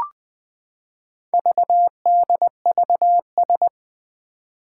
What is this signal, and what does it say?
Morse code, heard as a keyed tone, about 20 wpm, 700 Hz. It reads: VDVS
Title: Four Challenging Letter-Number Combinations 20wpm